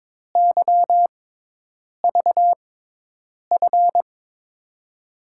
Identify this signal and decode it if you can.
Morse code, heard as a keyed tone, about 22 wpm, spaced as for 8 wpm, 700 Hz. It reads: YVF
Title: USA Call Signs - 6x Character Spacing - 22wpm